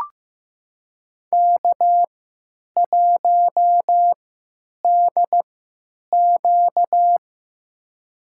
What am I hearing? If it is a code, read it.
Morse code, heard as a keyed tone, about 15 wpm, spaced as for 9 wpm, 700 Hz. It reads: K1DQ
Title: USA Call Signs - 4 Characters - 3x Character Spacing - 15wpm